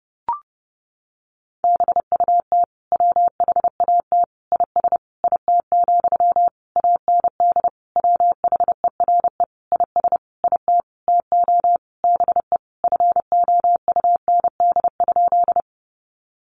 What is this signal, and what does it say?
Morse code, heard as a keyed tone, about 30 wpm, 700 Hz. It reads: BUT WHAT IS IT, AND WHERE IS IT TO BE FOUND?
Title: Sentences from Top 300 Words 30wpm